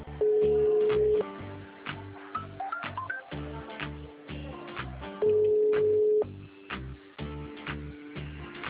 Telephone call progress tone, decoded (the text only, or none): ringback tone